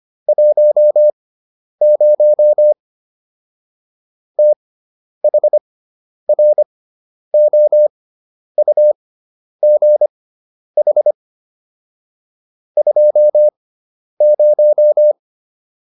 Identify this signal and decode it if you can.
Morse code, heard as a keyed tone, about 25 wpm, spaced as for 10 wpm, 600 Hz. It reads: 10 THROUGH 20